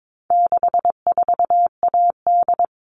Morse code, heard as a keyed tone, about 22 wpm, 700 Hz. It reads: 64AD
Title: All Cut Numbers - 4-Digit 22wpm